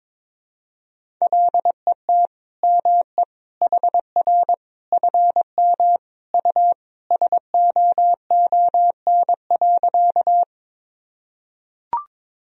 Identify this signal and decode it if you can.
Morse code, heard as a keyed tone, about 22 wpm, 700 Hz. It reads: LET ME HR FM U SOON.
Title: Sentences from Top 400 Words and Abbreviations 22wpm